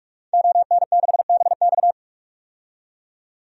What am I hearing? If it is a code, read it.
Morse code, heard as a keyed tone, about 42 wpm, 700 Hz. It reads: ON6BX
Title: DXCC-100 Call Signs - 5 Characters - 45wpm